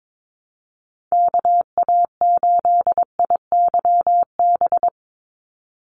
Morse code, heard as a keyed tone, about 22 wpm, 700 Hz. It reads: KA8IYB
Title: USA Call Signs 22wpm